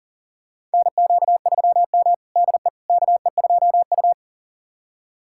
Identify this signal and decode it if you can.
Morse code, heard as a keyed tone, about 40 wpm, 700 Hz. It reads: NQ3M DE KE2U
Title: USA Callsign DE Callsign - 4 Characters - 40wpm